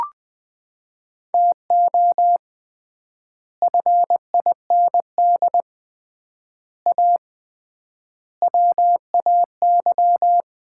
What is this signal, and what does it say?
Morse code, heard as a keyed tone, about 20 wpm, 700 Hz. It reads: TO FIND A WAY
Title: 3x Word-Spacing - Sets of 5 Words - 20wpm